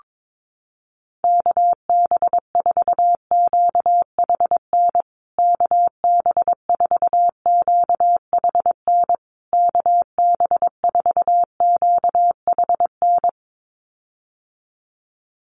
Morse code, heard as a keyed tone, about 22 wpm, 700 Hz. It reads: KB4QHN KB4QHN KB4QHN
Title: USA Call Signs Repeated 3x 22wpm